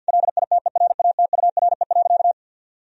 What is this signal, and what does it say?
Morse code, heard as a keyed tone, about 50 wpm, 700 Hz. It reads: LITERATURE.
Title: Sentences from Top 1000 Words - Review Part 3 - 50wpm